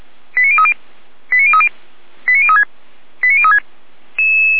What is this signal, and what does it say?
Five-tone selective call (ZVEI), heard as quick runs of five tones, tones of about 70 ms. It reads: 89030 89030 89036 89036